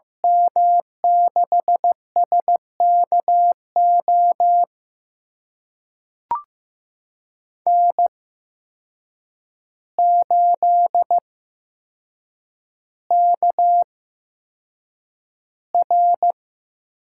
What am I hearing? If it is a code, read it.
Morse code, heard as a keyed tone, about 15 wpm, spaced as for 7 wpm, 700 Hz. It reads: M6SKO N8KR
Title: USA Call Signs - 8x Character Spacing - 15wpm